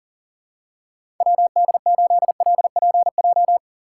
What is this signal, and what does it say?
Morse code, heard as a keyed tone, about 40 wpm, 700 Hz. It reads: WD8LPJ
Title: USA Call Signs - 6 Characters - 2x Character Spacing - 40wpm